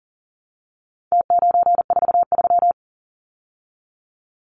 Morse code, heard as a keyed tone, about 40 wpm, 700 Hz. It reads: T943
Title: Numbers - Contest Style - 4-Digit 40wpm